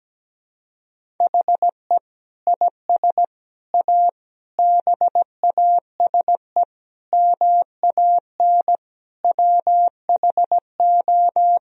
Morse code, heard as a keyed tone, about 17 wpm, 700 Hz. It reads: HE IS A BASE MAN WHO
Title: Sentences from Top 1000 Words - Review Part 1 - 17wpm